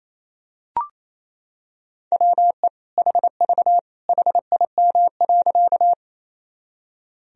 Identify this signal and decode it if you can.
Morse code, heard as a keyed tone, about 28 wpm, 700 Hz. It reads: WE HV HIM.
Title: Sentences from Top 100 Words and Abbreviations 28wpm